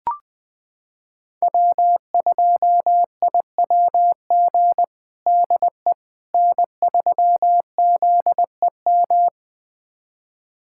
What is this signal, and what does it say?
Morse code, heard as a keyed tone, about 20 wpm, 700 Hz. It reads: W2IWG DE N3ZEM
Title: USA Callsign DE Callsign - 5 Characters - 20wpm